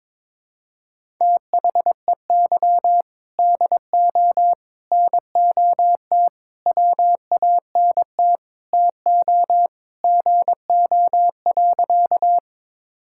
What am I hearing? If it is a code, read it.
Morse code, heard as a keyed tone, about 22 wpm, 700 Hz. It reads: THEY DO NOT WANT TO GO.
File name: practice / 22wpm / Sentences with 4-Word Phrases - 22wpm